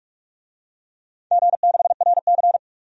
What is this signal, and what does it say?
Morse code, heard as a keyed tone, about 41 wpm, 700 Hz. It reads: G6RC
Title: DXCC-100 Call Signs - 4 Characters - 45wpm